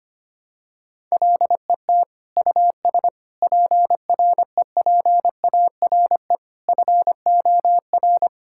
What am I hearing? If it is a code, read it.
Morse code, heard as a keyed tone, about 25 wpm, 700 Hz. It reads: LET US PREPARE FOR
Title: Sentences from Top 1000 Words - Review Part 6 - 25wpm